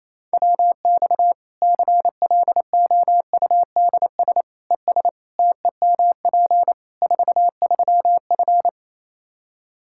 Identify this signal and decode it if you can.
Morse code, heard as a keyed tone, about 28 wpm, 700 Hz. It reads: WX CLOUDS ES TEMP 43F
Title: K1USN SST QSO Practice 28wpm